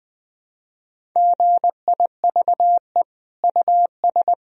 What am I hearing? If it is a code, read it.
Morse code, heard as a keyed tone, about 21 wpm, 700 Hz. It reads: GIVE US
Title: Sentences from Top 1000 Words - Review Part 1 - 20wpm